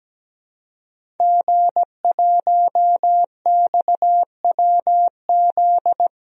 Morse code, heard as a keyed tone, about 17 wpm, 700 Hz. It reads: G1XWZ
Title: DXCC-100 Call Signs - 5 Characters - Repeated 3x - 17wpm